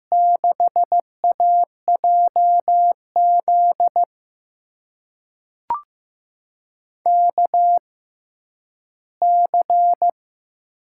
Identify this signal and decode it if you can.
Morse code, heard as a keyed tone, about 15 wpm, spaced as for 10 wpm, 700 Hz. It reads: 6AJZ KC
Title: USA Call Signs - 6 Characters - 6x Character Spacing - 15wpm